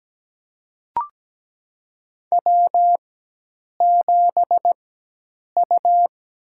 Morse code, heard as a keyed tone, about 17 wpm, spaced as for 8 wpm, 700 Hz. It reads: W7U
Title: USA Call Signs - 4x Character Spacing - 17wpm